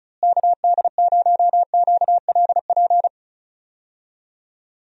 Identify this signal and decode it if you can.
Morse code, heard as a keyed tone, about 34 wpm, 700 Hz. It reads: KD0QLP
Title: USA Call Signs - 6 Characters - Repeated 3x - 35wpm